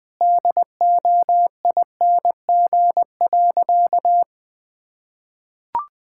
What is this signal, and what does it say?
Morse code, heard as a keyed tone, about 20 wpm, 700 Hz. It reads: DOING.
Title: Sentences with 8-Word Phrases - 20wpm